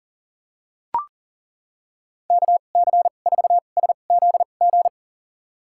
Morse code, heard as a keyed tone, about 40 wpm, spaced as for 29 wpm, 700 Hz. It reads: KC4SZG
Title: USA Call Signs - 6 Characters - 2x Character Spacing - 40wpm